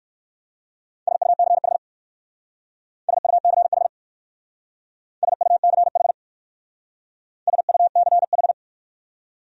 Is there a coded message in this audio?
Yes: Morse code from 1.1 to 8.5 s, heard as a keyed tone, about 57 wpm, 700 Hz. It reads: SUCH SUCH SUCH SUCH